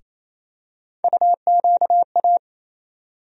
Morse code, heard as a keyed tone, about 27 wpm, 700 Hz. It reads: UQA
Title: USA Call Sign Postfixes - 3 Letters 28wpm